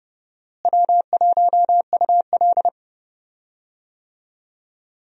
Morse code, heard as a keyed tone, about 30 wpm, 700 Hz. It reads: W1UL
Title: USA Call Signs - 4 Characters - Repeated 2x - 30wpm